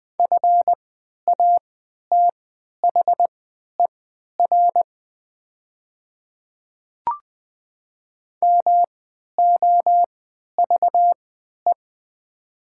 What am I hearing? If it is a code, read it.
Morse code, heard as a keyed tone, about 20 wpm, spaced as for 11 wpm, 700 Hz. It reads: FATHER MOVE
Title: 3x Character Spacing - Top 200 Words - 20wpm